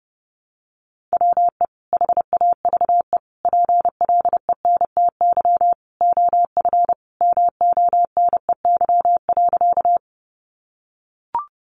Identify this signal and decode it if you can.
Morse code, heard as a keyed tone, about 30 wpm, 700 Hz. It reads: WE HAVE PLENTY OF MONEY.
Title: Sentences from Top 1300 Words 30wpm